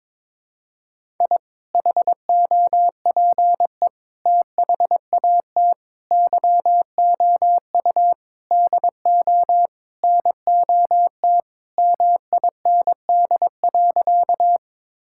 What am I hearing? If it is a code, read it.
Morse code, heard as a keyed tone, about 22 wpm, 700 Hz. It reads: I HOPE THAT YOU DO NOT MIND.
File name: practice / 22wpm / Sentences with 4-Word Phrases - 22wpm